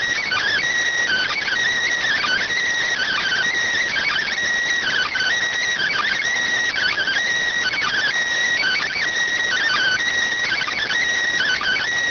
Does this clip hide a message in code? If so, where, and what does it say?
Yes: Emergency Alert System (SAME) tones (header bursts) from 0.5 to 10.9 s, heard as harsh buzzing bursts.